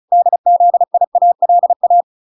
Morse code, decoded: DZIALA

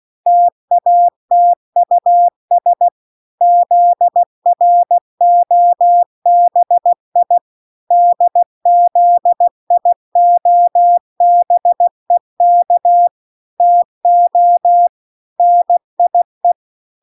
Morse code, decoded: TATUS ZROBI DZIOBEK TO NIE

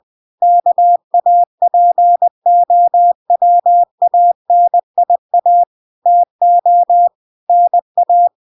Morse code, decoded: KAPOWANIA TO NA

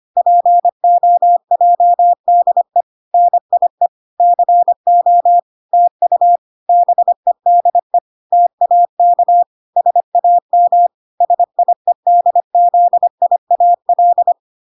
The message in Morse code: POJDE NIE CO TU BEDE TAK SAM SIEDZIAL

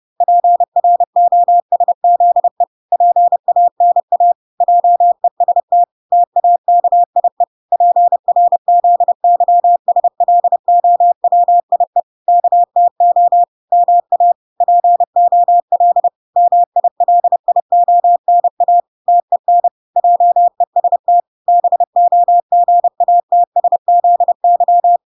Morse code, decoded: PROSZE PANA JEST TAKIE PRZYSLOWIE KTO MA POL MILIONA TEN JEST BOGATSZY

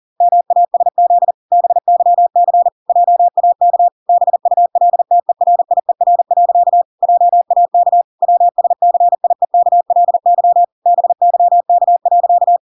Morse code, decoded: MASZ BYC JAK BULTERIER. JAK WSCIEKLY BYK.

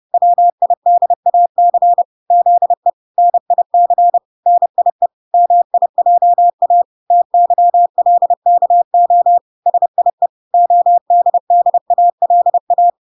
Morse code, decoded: WIDAC ZE NIC NIE MIJA TYLKO SIE ODDALA

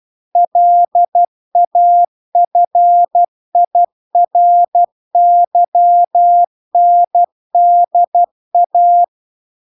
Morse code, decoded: LAFIRYNDA

